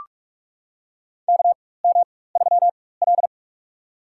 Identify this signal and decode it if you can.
Morse code, heard as a keyed tone, about 46 wpm, spaced as for 21 wpm, 700 Hz. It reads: KM3L
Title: USA Call Signs - 4 Characters - 4x Character Spacing - 45wpm